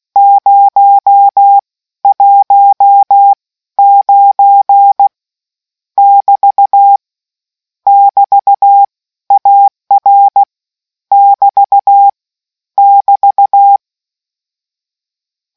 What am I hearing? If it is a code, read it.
Morse code, heard as a keyed tone, about 16 wpm, spaced as for 13 wpm, 800 Hz. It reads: 019 <BT> <BT> AR <BT> <BT>